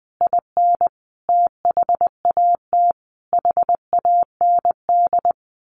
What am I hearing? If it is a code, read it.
Morse code, heard as a keyed tone, about 20 wpm, 700 Hz. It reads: IN THAT HAND
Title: A Sound Of Thunder 20wpm